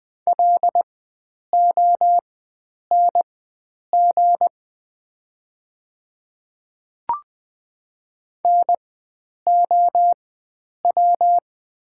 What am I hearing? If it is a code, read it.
Morse code, heard as a keyed tone, about 20 wpm, spaced as for 9 wpm, 700 Hz. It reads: LONG NOW
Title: 4x Character Spacing - Top 100 Words - 20wpm